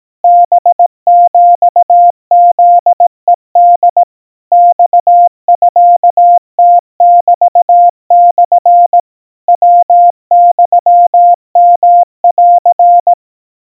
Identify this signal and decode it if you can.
Morse code (Japanese Wabun code), heard as a keyed tone, about 18 wpm, 700 Hz. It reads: ハヒフヘホ マミムメモ ヤユヨン
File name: 60 cpm 700 Hz Crispy